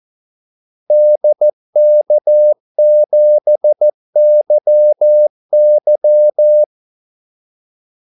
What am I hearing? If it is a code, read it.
Morse code, heard as a keyed tone, about 14 wpm, 600 Hz. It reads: DK7YY
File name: Callsigns-14wpm